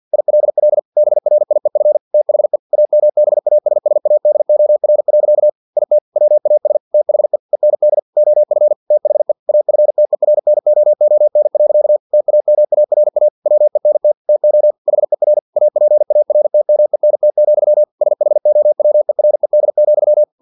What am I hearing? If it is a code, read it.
Morse code, heard as a keyed tone, about 50 wpm, 600 Hz. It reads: ILL BRIEF THE AMBASSADOR. IT WAS THE END OF THE AFTERNOON. TAMARA WENT TO HER APARTMENT, SHOWERED,
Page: Never 49wpm